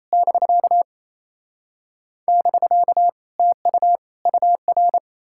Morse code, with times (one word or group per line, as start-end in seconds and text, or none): 0.1-0.8 <BK>
2.3-3.1 <BK>
3.4-4.0 TU
4.3-5.0 UR